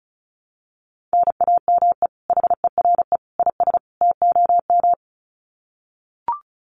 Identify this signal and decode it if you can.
Morse code, heard as a keyed tone, about 34 wpm, 700 Hz. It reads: NAME HERE IS TOM
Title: Common QSO Phrases 35wpm